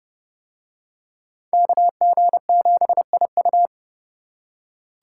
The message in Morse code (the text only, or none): KG7IU